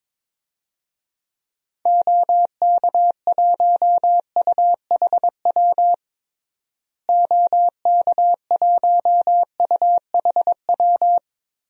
Morse code, heard as a keyed tone, about 22 wpm, 700 Hz. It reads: OK1UHW OK1UHW
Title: DXCC-100 Call Signs - 6 Characters - Repeated 3x - 22wpm